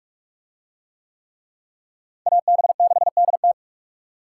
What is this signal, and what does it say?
Morse code, heard as a keyed tone, about 43 wpm, 700 Hz. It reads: AB6DT